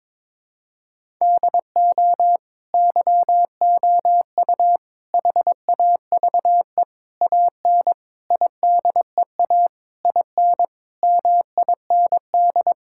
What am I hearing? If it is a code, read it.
Morse code, heard as a keyed tone, about 22 wpm, 700 Hz. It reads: DO YOU HAVE AN IDEA IN MIND
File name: Sets of 9 Words - 22wpm